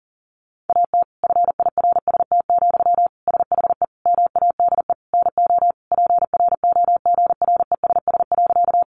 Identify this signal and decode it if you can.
Morse code, heard as a keyed tone, about 40 wpm, 700 Hz. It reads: AT FIRST, SHE MADE NO PROGRESS.
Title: Sentences from Top 1400 Words 40wpm